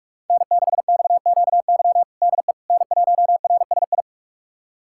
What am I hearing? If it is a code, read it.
Morse code, heard as a keyed tone, about 46 wpm, 700 Hz. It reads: N6XQY DE N1RSI
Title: USA Callsign DE Callsign - 5 Characters - 45wpm